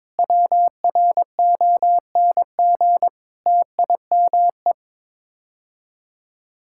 Morse code, heard as a keyed tone, about 22 wpm, 700 Hz. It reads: WRONG TIME
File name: practice / 22wpm / Sets of 8 Words - 22wpm